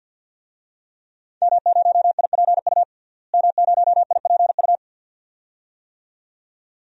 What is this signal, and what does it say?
Morse code, heard as a keyed tone, about 52 wpm, 700 Hz. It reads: M0IPU M0IPU
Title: DXCC-100 Call Signs - 5 Characters - Repeated 2x - 50wpm